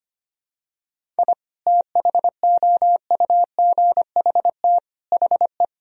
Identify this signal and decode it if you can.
Morse code, heard as a keyed tone, about 25 wpm, 700 Hz. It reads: I THOUGHT HE
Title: Sentences from Top 1000 Words - Review Part 1 - 25wpm